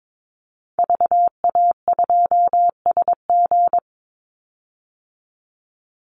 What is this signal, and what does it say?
Morse code, heard as a keyed tone, about 22 wpm, 700 Hz. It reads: VA2SG
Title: Call Signs Repeated 2x - 22wpm